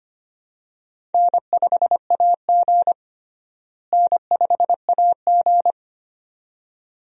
Morse code, heard as a keyed tone, about 25 wpm, 700 Hz. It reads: N5AG N5AG
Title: USA Call Signs - 4 Characters - Repeated 2x - 25wpm